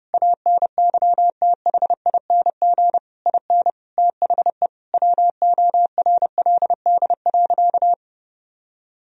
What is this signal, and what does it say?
Morse code, heard as a keyed tone, about 30 wpm, 700 Hz. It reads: ANYTHING IN THE WORLD.
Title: Sentences from Top 1000 Words - Review Part 3 - 30wpm